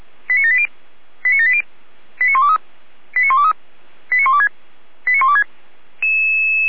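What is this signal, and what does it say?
Five-tone selective call (ZVEI), heard as quick runs of five tones, tones of about 70 ms. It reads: 89780 89780 89123 89123 89126 89126
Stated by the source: ZVEI five-tone selective call received by SDR, from the Feuerwehr Orlamünde fire brigade audio server.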